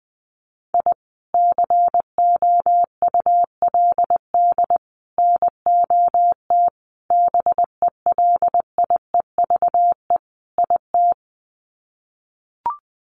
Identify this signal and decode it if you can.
Morse code, heard as a keyed tone, about 20 wpm, 700 Hz. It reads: I COULD NOT BELIEVE IT
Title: Sets of 5 Words Encore 20wpm